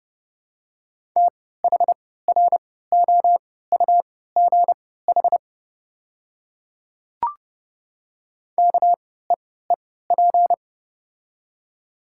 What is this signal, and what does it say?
Morse code, heard as a keyed tone, about 31 wpm, spaced as for 17 wpm, 700 Hz. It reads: THROUGH KEEP
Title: 3x Character Spacing - Top 300 Words - 30wpm